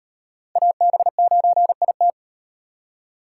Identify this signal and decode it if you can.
Morse code, heard as a keyed tone, about 36 wpm, 700 Hz. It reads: AB9IT